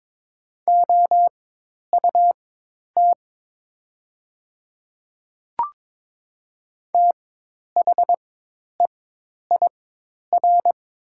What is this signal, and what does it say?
Morse code, heard as a keyed tone, about 22 wpm, spaced as for 10 wpm, 700 Hz. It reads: OUT THEIR
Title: 4x Character Spacing - Top 100 Words - 22wpm